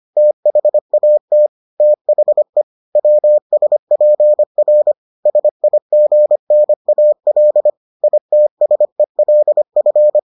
Morse code, heard as a keyed tone, about 25 wpm, 600 Hz. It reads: THAT THE WSPR SIGNAL ITSELF